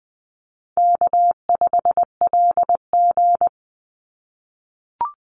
Morse code, heard as a keyed tone, about 20 wpm, 700 Hz. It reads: K5LG
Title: Call Signs - 20wpm